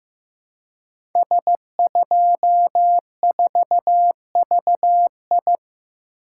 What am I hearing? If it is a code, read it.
Morse code, heard as a keyed tone, about 15 wpm, 700 Hz. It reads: S24VI
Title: Code Groups - 5 Letter:Numbers - 15wpm